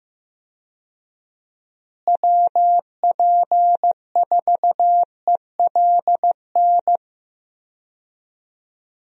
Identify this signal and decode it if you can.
Morse code, heard as a keyed tone, about 15 wpm, 700 Hz. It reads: WP4ELN
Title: USA Call Signs - 6 Characters - Repeated 3x - 15wpm